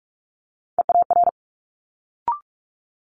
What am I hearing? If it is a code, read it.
Morse code, heard as a keyed tone, about 46 wpm, 700 Hz. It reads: EAR
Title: Multi-letter Phonemes Quick Fire 45wpm